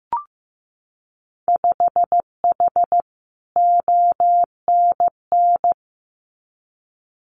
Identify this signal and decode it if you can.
Morse code, heard as a keyed tone, about 15 wpm, 700 Hz. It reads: 5H ONN
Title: Winter Field Day Exchanges Encore 15wpm